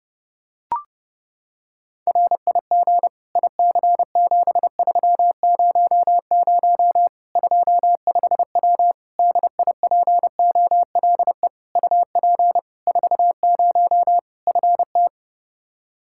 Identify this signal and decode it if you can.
Morse code, heard as a keyed tone, about 30 wpm, 700 Hz. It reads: RIG IC7300 25W DIPOLE UP 40 FT
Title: K1USN SST QSO Practice 30wpm